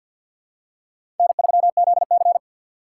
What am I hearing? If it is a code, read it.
Morse code, heard as a keyed tone, about 53 wpm, 700 Hz. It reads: N3ZC
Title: USA Call Signs - 4 Characters - 6x Character Spacing - 50wpm